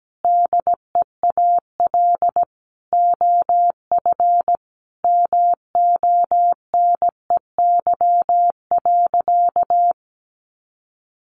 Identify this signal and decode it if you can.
Morse code, heard as a keyed tone, about 17 wpm, 700 Hz. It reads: DEAL OF MONEY.
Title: Sentences from Top 1300 Words 17wpm